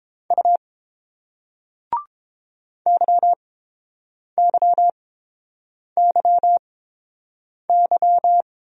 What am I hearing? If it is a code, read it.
Morse code, heard as a keyed tone, about 30 wpm, spaced as for 7 wpm, 700 Hz. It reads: U YYYY